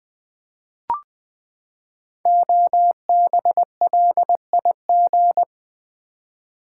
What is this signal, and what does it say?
Morse code, heard as a keyed tone, about 20 wpm, 700 Hz. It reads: OBLIG
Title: Build A Word - 5 Letters At A Time - 10-Letter Words - 20wpm